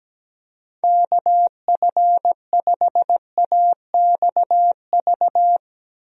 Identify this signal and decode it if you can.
Morse code, heard as a keyed tone, about 17 wpm, 700 Hz. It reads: KF5AXV
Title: USA Call Signs - 6 Characters - 17wpm